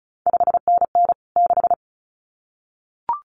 Morse code, heard as a keyed tone, about 36 wpm, 700 Hz. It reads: 5NN 6
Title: CQ WW DX Contest Exchange Encore - 35wpm